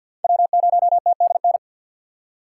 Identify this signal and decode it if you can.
Morse code, heard as a keyed tone, about 52 wpm, 700 Hz. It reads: W0TDN
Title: USA Call Signs - 5 Characters - 2x Character Spacing - 50wpm